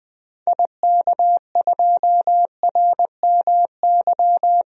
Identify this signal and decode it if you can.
Morse code, heard as a keyed tone, about 20 wpm, 700 Hz. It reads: IK2RMY
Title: DXCC-100 Call Signs - 6 Characters - 20wpm